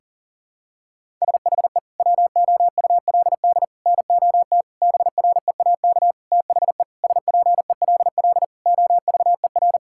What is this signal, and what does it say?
Morse code, heard as a keyed tone, about 40 wpm, 700 Hz. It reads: SHE WOULD NOT BREAK THE SPELL OVER
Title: Sentences from Top 1000 Words - Review Part 6 - 40wpm